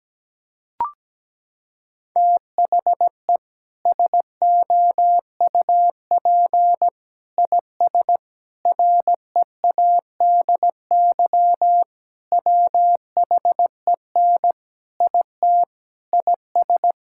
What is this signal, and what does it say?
Morse code, heard as a keyed tone, about 17 wpm, 700 Hz. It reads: THE SOUP IS READY WHEN IT IS
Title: Sentences from Top 1600 Words 17wpm